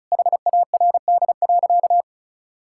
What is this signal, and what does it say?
Morse code, heard as a keyed tone, about 36 wpm, 700 Hz. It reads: HARD.
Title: Pangram Challenge v2 - 35wpm